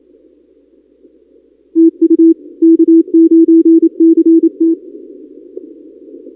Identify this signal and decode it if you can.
Morse code, heard as a keyed tone, about 28 wpm, 330 Hz. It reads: TU K9CT
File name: contests / 2019-cw-wpx-kt5j-k5pi- -2330 0-32